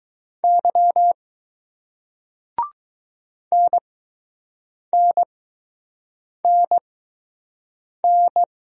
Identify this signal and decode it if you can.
Morse code, heard as a keyed tone, about 20 wpm, spaced as for 6 wpm, 700 Hz. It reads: Y NNNN